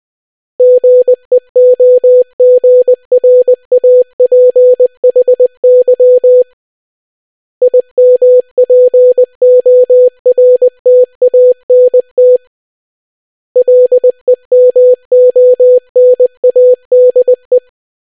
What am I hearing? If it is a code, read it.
Morse code, heard as a keyed tone, about 20 wpm, 500 Hz. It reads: GEOGRAPHY IMPORTANT LEMONADE